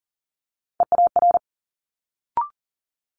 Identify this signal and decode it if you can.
Morse code, heard as a keyed tone, about 41 wpm, 700 Hz. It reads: EAR
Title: Multi-letter Phonemes Quick Fire 40wpm